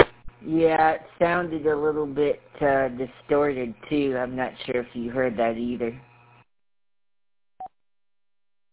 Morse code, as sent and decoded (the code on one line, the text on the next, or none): none